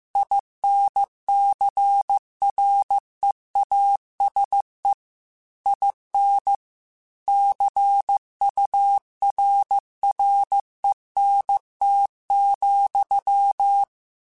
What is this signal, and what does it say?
Morse code, heard as a keyed tone, about 15 wpm, 800 Hz. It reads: INCREASE IN CURRENT,
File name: Technical Subjects / Technical Subjects-mp3s / XL @13